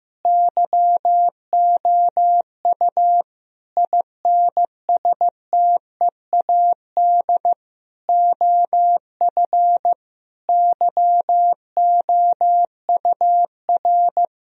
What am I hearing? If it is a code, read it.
Morse code, heard as a keyed tone, about 15 wpm, 700 Hz. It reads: YOU INSTEAD OF YOUR
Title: Sentences from Top 1000 Words - Review Part 4 - 15wpm